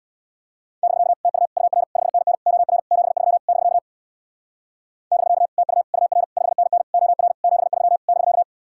Morse code, heard as a keyed tone, about 60 wpm, 700 Hz. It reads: <BK> TU UR 5NN OR 73 <BK> <BK> TU UR 5NN OR 73 <BK>
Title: Speed-Racing Common QSO Phrases - 50wpm